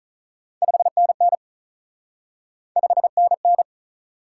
Morse code, heard as a keyed tone, about 46 wpm, 700 Hz. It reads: 5NN 5NN